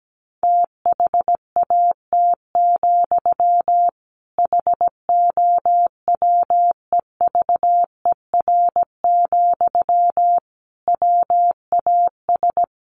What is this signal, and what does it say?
Morse code, heard as a keyed tone, about 17 wpm, 700 Hz. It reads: THAT, HOWEVER, WAS
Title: Sentences from Top 600 Words 17wpm